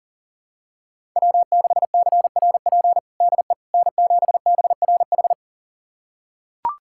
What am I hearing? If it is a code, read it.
Morse code, heard as a keyed tone, about 40 wpm, 700 Hz. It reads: W6CRP DE N7BRH
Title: USA Callsign DE Callsign - 5 Characters - 40wpm